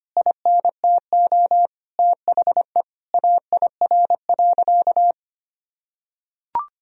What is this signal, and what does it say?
Morse code, heard as a keyed tone, about 25 wpm, 700 Hz. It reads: INTO THE AIR.
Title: Sentences from Top 1000 Words - Review Part 1 - 25wpm